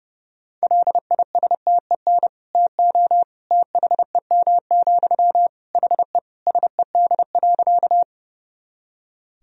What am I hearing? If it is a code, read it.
Morse code, heard as a keyed tone, about 29 wpm, 700 Hz. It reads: LISTEN TO THEM, HE SED.
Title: Sentences from Top 400 Words and Abbreviations 30wpm